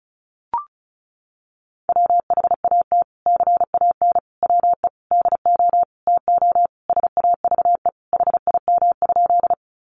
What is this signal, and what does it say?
Morse code, heard as a keyed tone, about 34 wpm, 700 Hz. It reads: WHAT CAN WE DO TO SAVE HIM?
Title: Sentences from Top 1100 Words 35wpm